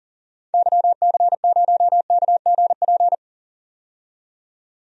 Morse code, heard as a keyed tone, about 40 wpm, 700 Hz. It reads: YC0KGP
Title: DXCC-100 Call Signs - 6 Characters - 40wpm